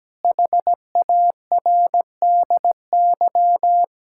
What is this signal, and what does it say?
Morse code, heard as a keyed tone, about 17 wpm, 700 Hz. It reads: HARDY
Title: Pangram Challenge v2 - 17wpm